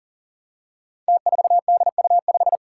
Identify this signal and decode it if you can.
Morse code, heard as a keyed tone, about 41 wpm, 700 Hz. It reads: T4DU5
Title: Five Challenging Letter-Number Combinations 40wpm